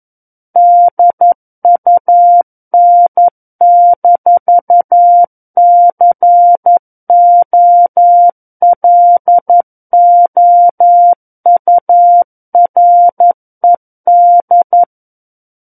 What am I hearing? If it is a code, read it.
Morse code, heard as a keyed tone, about 11 wpm, 700 Hz. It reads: DUN-COLOURED
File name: Book One / 02-Chapter 2 - 11 WPM